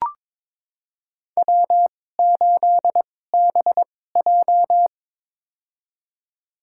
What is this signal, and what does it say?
Morse code, heard as a keyed tone, about 22 wpm, spaced as for 16 wpm, 700 Hz. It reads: W8BJ
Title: USA Call Signs - 4 Characters - 2x Character Spacing - 22wpm